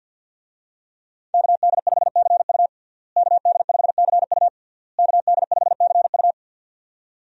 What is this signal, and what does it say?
Morse code, heard as a keyed tone, about 47 wpm, 700 Hz. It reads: KD5CU KD5CU KD5CU
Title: USA Call Signs - 5 Characters - Repeated 3x - 50wpm